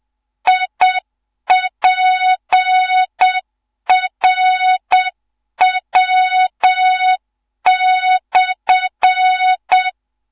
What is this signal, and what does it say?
Morse code, heard as a keyed tone, about 7 wpm, 3700 Hz, with noise. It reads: IPRW/
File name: Morze-HA2MN / hang / 15 GYAKORLAT